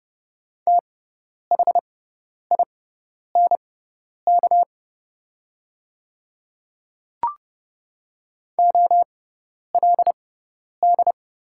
Morse code, heard as a keyed tone, about 30 wpm, spaced as for 10 wpm, 700 Hz. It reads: THINK OLD
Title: 6x Character Spacing - Top 200 Words - 30wpm